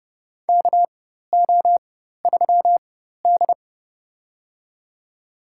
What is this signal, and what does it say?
Morse code, heard as a keyed tone, about 32 wpm, spaced as for 14 wpm, 700 Hz. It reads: KO3D
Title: USA Call Signs - 4 Characters - 4x Character Spacing - 30wpm